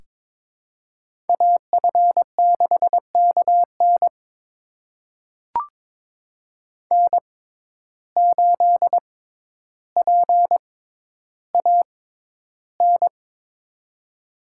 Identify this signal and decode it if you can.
Morse code, heard as a keyed tone, about 21 wpm, spaced as for 11 wpm, 700 Hz. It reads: AF6KN N8PAN
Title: USA Call Signs - 5 Characters - 6x Character Spacing - 22wpm